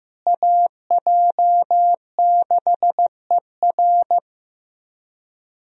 Morse code, heard as a keyed tone, about 15 wpm, 700 Hz. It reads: AJ6ER
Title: USA Call Signs - 5 Characters - Repeated 3x - 15wpm